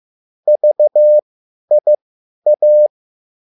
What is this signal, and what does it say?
Morse code, heard as a keyed tone, about 15 wpm, spaced as for 10 wpm, 600 Hz. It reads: VIA